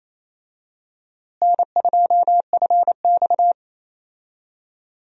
Morse code, heard as a keyed tone, about 28 wpm, 700 Hz. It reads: N2FX
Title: USA Call Signs - 4 Characters - Repeated 2x - 28wpm